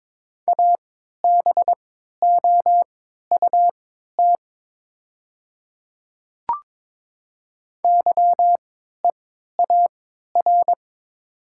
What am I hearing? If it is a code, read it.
Morse code, heard as a keyed tone, about 22 wpm, spaced as for 12 wpm, 700 Hz. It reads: ABOUT YEAR
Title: 3x Character Spacing - Top 200 Words - 22wpm